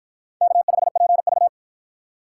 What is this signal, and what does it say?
Morse code, heard as a keyed tone, about 44 wpm, 700 Hz. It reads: K5PV